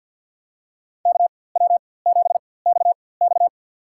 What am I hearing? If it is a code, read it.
Morse code, heard as a keyed tone, about 52 wpm, spaced as for 23 wpm, 700 Hz. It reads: KW7XX
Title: USA Call Signs - 5 Characters - 4x Character Spacing - 50wpm